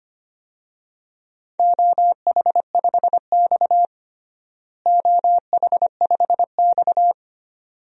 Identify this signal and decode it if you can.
Morse code, heard as a keyed tone, about 25 wpm, 700 Hz. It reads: OH5X OH5X
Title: DXCC-100 Call Signs - 4 Characters - Repeated 2x - 25wpm